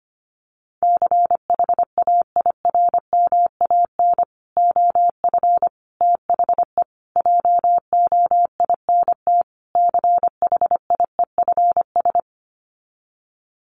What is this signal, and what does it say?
Morse code, heard as a keyed tone, about 25 wpm, 700 Hz. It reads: CHAIRMAN OF THE JOINT CHIEFS
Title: Sets of 5 Words Encore 25wpm